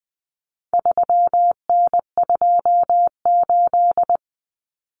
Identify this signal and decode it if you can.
Morse code, heard as a keyed tone, about 20 wpm, 700 Hz. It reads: 3N28
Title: Common Cut Numbers - 4-Digit 20wpm